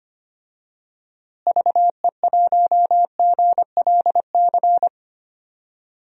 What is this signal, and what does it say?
Morse code, heard as a keyed tone, about 25 wpm, 700 Hz. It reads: VE1GLC
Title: DXCC-100 Call Signs - 6 Characters - Repeated 2x - 25wpm